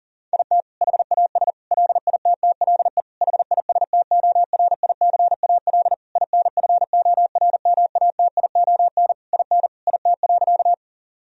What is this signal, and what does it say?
Morse code, heard as a keyed tone, about 40 wpm, 700 Hz. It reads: IT HAS LITTLE HISTORICAL INFORMATION IN IT.
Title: Sentences from Top 1000 Words - Review Part 4 - 40wpm